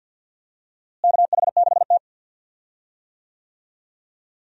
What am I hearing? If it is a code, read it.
Morse code, heard as a keyed tone, about 53 wpm, 700 Hz. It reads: KH6T
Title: DXCC-100 Call Signs - 4 Characters - Repeated 2x - 50wpm